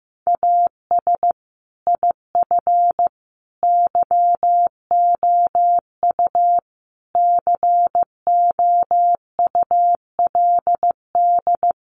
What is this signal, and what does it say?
Morse code, heard as a keyed tone, about 15 wpm, 700 Hz. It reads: AS IF YOU COULD
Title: Sentences from Top 900 Words 15wpm